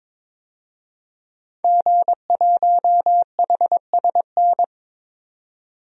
Morse code, heard as a keyed tone, about 21 wpm, 700 Hz. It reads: G1HSN
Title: DXCC-100 Call Signs - 5 Characters - 22wpm